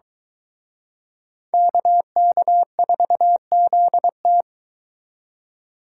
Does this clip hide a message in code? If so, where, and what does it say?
Yes: Morse code from 1.5 to 4.4 s, heard as a keyed tone, about 23 wpm, 700 Hz. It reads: KK4ZT